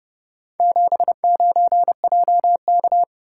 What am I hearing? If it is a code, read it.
Morse code, heard as a keyed tone, about 29 wpm, 700 Hz. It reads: 79JK